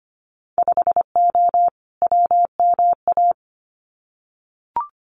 Morse code, heard as a keyed tone, about 25 wpm, 690 Hz. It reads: 5O WMA
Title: Winter Field Day Exchanges 25wpm